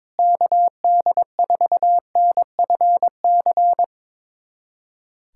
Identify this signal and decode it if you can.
Morse code, heard as a keyed tone, about 22 wpm, 700 Hz. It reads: KD4NFC
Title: USA Callsign DE Callsign - 6 Characters - 22wpm